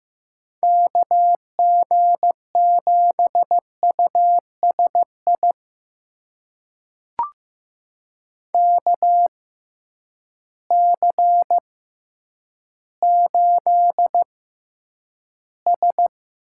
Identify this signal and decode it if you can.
Morse code, heard as a keyed tone, about 15 wpm, spaced as for 9 wpm, 700 Hz. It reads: KG7USI KC8S
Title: USA Call Signs - 6 Characters - 6x Character Spacing - 15wpm